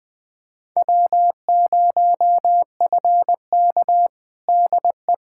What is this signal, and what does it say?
Morse code, heard as a keyed tone, about 21 wpm, 700 Hz. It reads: W0FK DE
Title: USA Callsign DE Callsign - 4 Characters - 20wpm